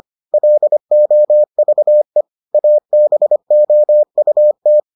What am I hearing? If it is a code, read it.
Morse code, heard as a keyed tone, about 25 wpm, 600 Hz. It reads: LOVE ABOUT